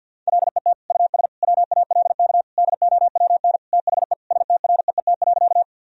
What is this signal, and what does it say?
Morse code, heard as a keyed tone, about 48 wpm, 700 Hz. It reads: LET US WALK DOWN THE STREET.
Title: Sentences from Top 1000 Words - Review Part 3 - 50wpm